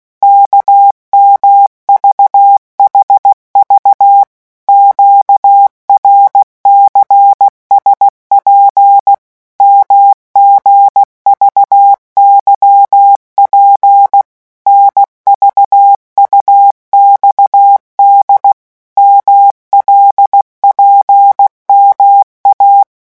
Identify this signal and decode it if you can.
Morse code, heard as a keyed tone, about 16 wpm, 800 Hz. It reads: KMVHV QRCSP MGVYP NVUXD MLPMA
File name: morse mp3 / revision5-vitesse 15 mots